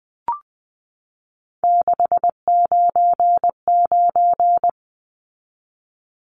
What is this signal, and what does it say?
Morse code, heard as a keyed tone, about 20 wpm, 700 Hz. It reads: 699
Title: Numbers - 3-Digit 20wpm